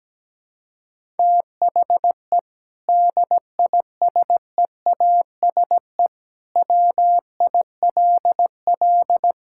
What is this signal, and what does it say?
Morse code, heard as a keyed tone, about 17 wpm, 700 Hz. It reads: THE DISEASE WILL
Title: Sentences from Top 1000 Words - Review Part 6 - 17wpm